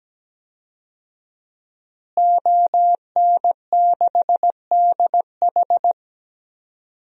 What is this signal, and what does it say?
Morse code, heard as a keyed tone, about 17 wpm, 700 Hz. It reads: ON6DH
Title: DXCC-100 Call Signs - 5 Characters - Repeated 3x - 17wpm